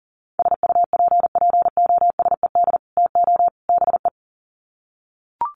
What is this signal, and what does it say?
Morse code, heard as a keyed tone, about 40 wpm, 700 Hz. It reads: SUPPOSED TO BE
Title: Sets of 3 Words 40wpm